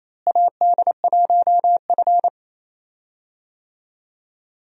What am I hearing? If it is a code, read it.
Morse code, heard as a keyed tone, about 28 wpm, 700 Hz. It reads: AD1F
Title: USA Call Signs - 4 Characters - Repeated 2x - 28wpm